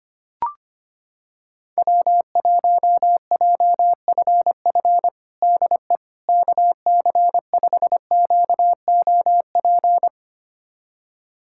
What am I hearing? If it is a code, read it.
Morse code, heard as a keyed tone, about 25 wpm, 700 Hz. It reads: W1JFF DE KC5QOP
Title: USA Callsign DE Callsign 25wpm